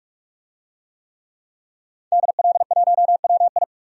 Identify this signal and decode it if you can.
Morse code, heard as a keyed tone, about 44 wpm, 700 Hz. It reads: DL1WI
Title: DXCC-100 Call Signs - 5 Characters - 45wpm